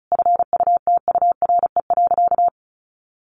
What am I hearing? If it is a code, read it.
Morse code, heard as a keyed tone, about 34 wpm, 700 Hz. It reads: FUTURE.
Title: Sentences from Top 700 Words 35wpm